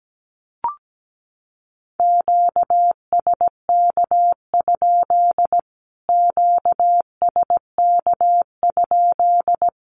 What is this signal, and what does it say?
Morse code, heard as a keyed tone, about 17 wpm, 700 Hz. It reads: QSK? QSK?
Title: Q-Codes Repeated 3x 17wpm